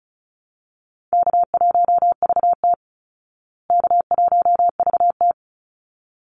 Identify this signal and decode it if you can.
Morse code, heard as a keyed tone, about 34 wpm, 700 Hz. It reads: K1VT K1VT
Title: Easy Call Signs Repeated 2x - 35wpm